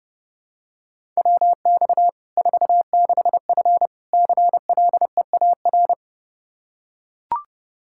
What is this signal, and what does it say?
Morse code, heard as a keyed tone, about 30 wpm, 700 Hz. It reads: WX 46F CLEAR
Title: Practice QSO - Exchange 2 - Info - 30wpm